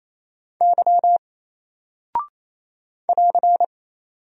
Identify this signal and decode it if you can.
Morse code, heard as a keyed tone, about 27 wpm, 700 Hz. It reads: Y <AR>
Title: Single Character (Letter:Number:Punctuation:Prosign) Alt 28wpm